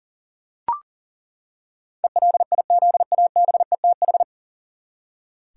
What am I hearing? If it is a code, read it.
Morse code, heard as a keyed tone, about 40 wpm, 700 Hz. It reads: ELIZABETH
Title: USA Common Names - 40wpm